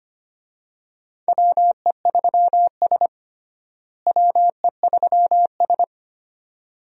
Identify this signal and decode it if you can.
Morse code, heard as a keyed tone, about 25 wpm, 700 Hz. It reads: WE3S WE3S
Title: USA Call Signs - 4 Characters - Repeated 2x - 25wpm